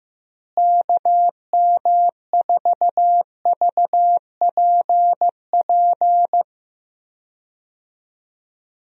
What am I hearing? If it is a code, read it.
Morse code, heard as a keyed tone, about 14 wpm, 700 Hz. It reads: KM4VPP
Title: USA Call Signs - 6 Characters - Repeated 3x - 15wpm